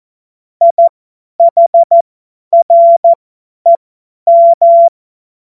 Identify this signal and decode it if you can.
Morse code, heard as a keyed tone, about 14 wpm, spaced as for 10 wpm, 680 Hz. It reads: IHREM
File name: Morsekurs / Witze / Witz 03. Januar 2023 50 F70